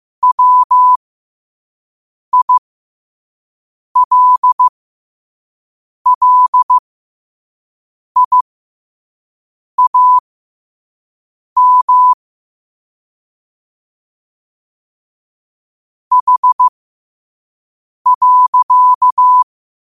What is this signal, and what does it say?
Morse code, heard as a keyed tone, about 15 wpm, spaced as for 5 wpm, 1000 Hz. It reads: WILLIAM H.
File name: qotdfiles / 5wpm